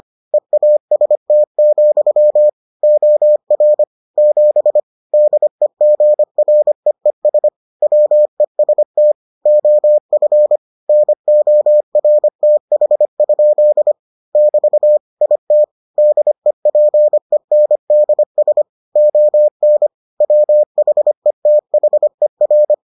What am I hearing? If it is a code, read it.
Morse code, heard as a keyed tone, about 25 wpm, 600 Hz. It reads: EAST, OR 7 DEGREES WEST OF NORTH? <BT> IT DEPENDS ON WHETHER